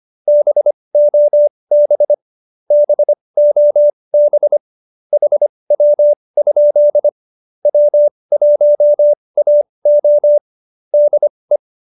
Morse code, heard as a keyed tone, about 25 wpm, spaced as for 20 wpm, 590 Hz. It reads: BOB BOB HW? W1AO DE
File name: QSO313 20